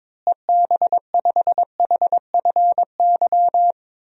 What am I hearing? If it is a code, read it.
Morse code, heard as a keyed tone, about 22 wpm, 700 Hz. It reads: EB5HFY
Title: DXCC-100 Call Signs - 6 Characters - Repeated 2x - 22wpm